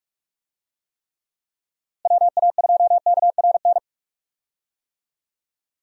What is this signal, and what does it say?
Morse code, heard as a keyed tone, about 46 wpm, 700 Hz. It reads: WA2KRN
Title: USA Call Signs - 6 Characters - Repeated 2x - 45wpm